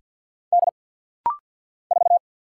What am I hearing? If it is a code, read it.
Morse code, heard as a keyed tone, about 50 wpm, 700 Hz. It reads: D 4
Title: Single Letter-Number - Rapid-Fire 50wpm